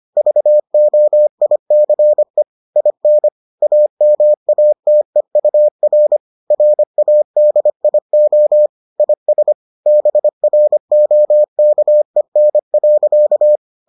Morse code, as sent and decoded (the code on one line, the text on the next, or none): ...- --- .. -.-. / .. -. / .- -- .- - . ..- .-. / .-. .- -.. .. --- / .. ... / -... .-. --- -.- . -. .-.-.-
VOICE IN AMATEUR RADIO IS BROKEN.